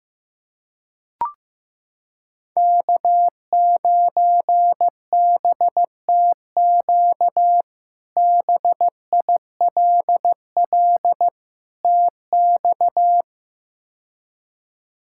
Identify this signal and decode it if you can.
Morse code, heard as a keyed tone, about 15 wpm, 700 Hz. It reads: K9BTQ BILL TX
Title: K1USN SST QSO Practice - 15wpm